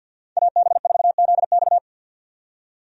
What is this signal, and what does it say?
Morse code, heard as a keyed tone, about 50 wpm, 700 Hz. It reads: AB4ZX
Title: USA Call Signs - 5 Characters - 50wpm